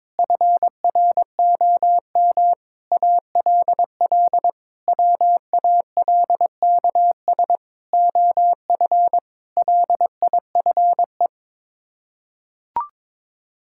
Morse code, heard as a keyed tone, about 22 wpm, 700 Hz. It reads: FROM ALL WALKS OF LIFE
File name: practice / 22wpm / Sets of 6 Words - 22wpm